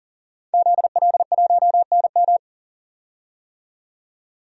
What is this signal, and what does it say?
Morse code, heard as a keyed tone, about 39 wpm, 700 Hz. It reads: ZL1NM
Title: DXCC-100 Call Signs - 5 Characters - 40wpm